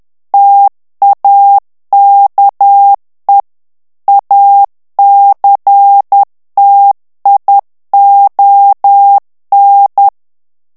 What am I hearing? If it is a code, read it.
Morse code, heard as a keyed tone, about 10 wpm, 800 Hz. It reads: TAKE ACTION
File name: ANGLAIS texte clair 3 vitesse 10 mots minute